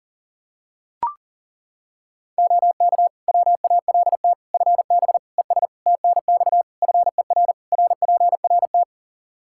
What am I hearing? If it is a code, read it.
Morse code, heard as a keyed tone, about 40 wpm, 700 Hz. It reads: OK WALT FB ES TNX FER RPRT
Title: Practice QSO - Exchange 2 - Info - 40wpm